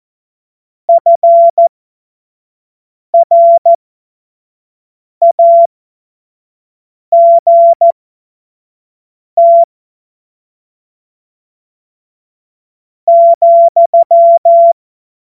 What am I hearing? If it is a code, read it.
Morse code, heard as a keyed tone, about 14 wpm, spaced as for 4 wpm, 680 Hz. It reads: FRAGT,
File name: Morsekurs / Witze / Witz 07. September 2021, 25 F70